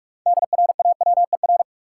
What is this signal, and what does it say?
Morse code, heard as a keyed tone, about 46 wpm, 700 Hz. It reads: DRAWER
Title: Six Letter Words - 45wpm